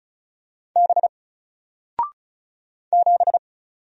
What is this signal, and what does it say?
Morse code, heard as a keyed tone, about 36 wpm, 700 Hz. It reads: B 7
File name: Single-Letter-Number-35wpm